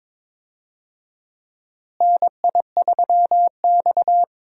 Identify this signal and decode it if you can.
Morse code, heard as a keyed tone, about 22 wpm, 700 Hz. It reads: NI3X
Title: USA Call Signs - 4 Characters - Repeated 2x - 22wpm